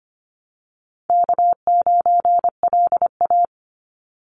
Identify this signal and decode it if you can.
Morse code, heard as a keyed tone, about 25 wpm, 700 Hz. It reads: K9LA
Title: Easy Call Signs Repeated 2x - 25wpm